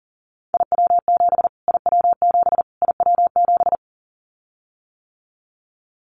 Morse code, heard as a keyed tone, about 40 wpm, 700 Hz. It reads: IW7 IW7 IW7
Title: Callsign Prefixes for Top 25 DXCC - Repeated 3x - 40wpm